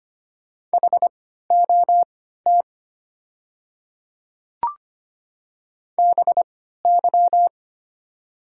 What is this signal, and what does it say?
Morse code, heard as a keyed tone, about 25 wpm, spaced as for 14 wpm, 700 Hz. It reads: HOT BY
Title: 3x Character Spacing - Top 200 Words - 25wpm